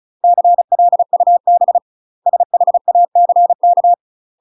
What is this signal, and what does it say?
Morse code, heard as a keyed tone, about 35 wpm, 700 Hz. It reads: CLUB SHACK
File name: UK MASTER QSOs TRAINING / 35WPM / UK MASTER QSOs 0038